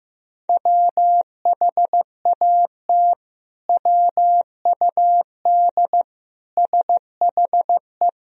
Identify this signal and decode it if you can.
Morse code, heard as a keyed tone, about 15 wpm, 700 Hz. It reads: WHAT WUD SHE
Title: Sentences from Top 100 Words and Abbreviations 15wpm